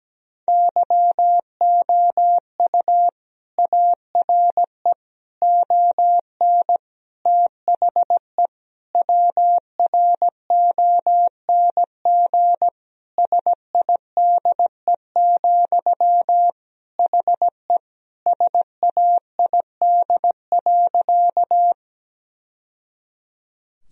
Morse code, heard as a keyed tone, about 17 wpm, 700 Hz. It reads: YOU ARE ON THE WRONG SIDE, HE SAID.
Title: Sentences from Top 1000 Words - Review Part 1 - 17wpm